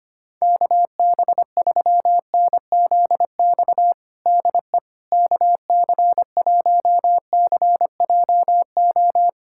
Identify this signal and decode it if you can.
Morse code, heard as a keyed tone, about 25 wpm, 700 Hz. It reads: KB3NZX DE KC1CJO
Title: USA Callsign DE Callsign 25wpm